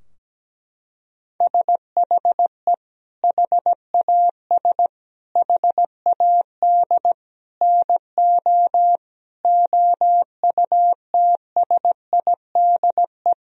Morse code, heard as a keyed tone, about 17 wpm, 700 Hz. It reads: SHE HAS HAD NO OUTSIDE
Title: Sentences from Top 1000 Words - Review Part 4 - 17wpm